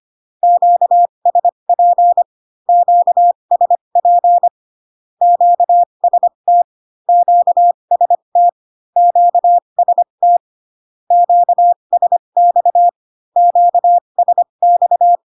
Morse code, heard as a keyed tone, about 25 wpm, spaced as for 22 wpm, 700 Hz. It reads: QSP QSP QST QST QST QSX QSX
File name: Q-codes most common repeated 3 times 18 25wpm